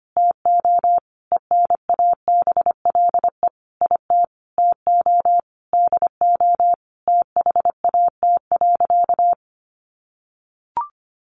Morse code, heard as a keyed tone, about 25 wpm, 700 Hz. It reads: TO ENABLE IT TO DO THAT.
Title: Sentences from Top 1500 Words 25wpm